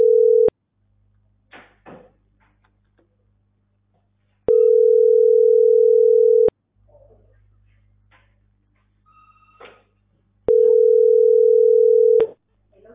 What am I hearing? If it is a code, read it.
Ringback tone (North America), 440 + 480 Hz.